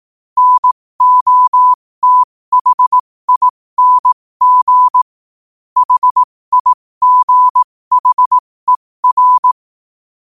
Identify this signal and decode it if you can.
Morse code, heard as a keyed tone, about 18 wpm, spaced as for 15 wpm, 1000 Hz. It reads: NOTHING HIGHER
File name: qotdfiles / 15wpm